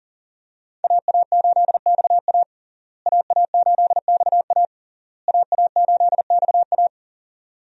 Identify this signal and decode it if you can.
Morse code, heard as a keyed tone, about 40 wpm, 700 Hz. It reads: AA8XA AA8XA AA8XA
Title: USA Call Signs - 5 Characters - Repeated 3x - 40wpm